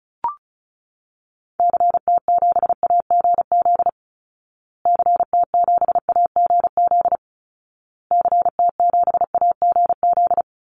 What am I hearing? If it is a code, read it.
Morse code, heard as a keyed tone, about 34 wpm, 700 Hz. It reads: CT7AGZ CT7AGZ CT7AGZ
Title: Call Signs Repeated 2x - 35wpm